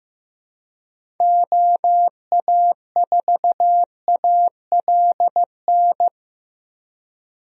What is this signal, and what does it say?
Morse code, heard as a keyed tone, about 15 wpm, 700 Hz. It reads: OA4ALN
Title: USA Call Signs - 6 Characters - Repeated 2x - 15wpm